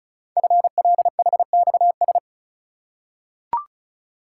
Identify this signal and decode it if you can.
Morse code, heard as a keyed tone, about 36 wpm, 700 Hz. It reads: FLHXS
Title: Code Groups - 5 Letters - 35wpm